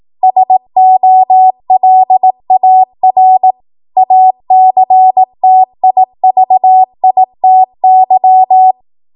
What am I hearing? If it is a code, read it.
Morse code, heard as a keyed tone, about 18 wpm, 750 Hz. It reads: SOLAR ACTIVITY
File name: Morse / 18WPM